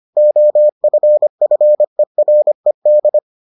Morse code, heard as a keyed tone, about 25 wpm, 600 Hz. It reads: OFFERED